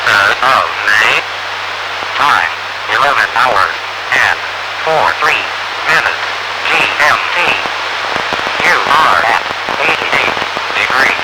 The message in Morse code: none